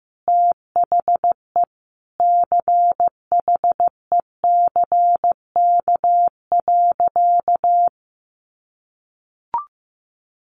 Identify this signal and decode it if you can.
Morse code, heard as a keyed tone, about 15 wpm, 700 Hz. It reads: THE CHECK.
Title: A Sound Of Thunder 15wpm